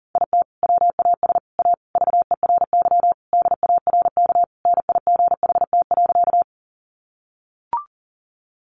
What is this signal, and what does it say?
Morse code, heard as a keyed tone, about 40 wpm, 700 Hz. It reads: IT WAS A VERY DARK NIGHT.
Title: Sentences from Top 500 Words 40wpm